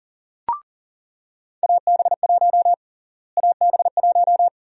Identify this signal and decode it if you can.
Morse code, heard as a keyed tone, about 40 wpm, 700 Hz. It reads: AB1 AB1
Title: USA Call Sign Prefixes - Repeated 2x - 40wpm